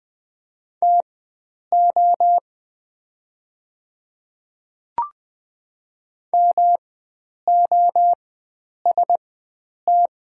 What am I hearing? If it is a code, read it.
Morse code, heard as a keyed tone, about 20 wpm, spaced as for 9 wpm, 700 Hz. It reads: TO MOST
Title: 4x Character Spacing - Top 100 Words - 20wpm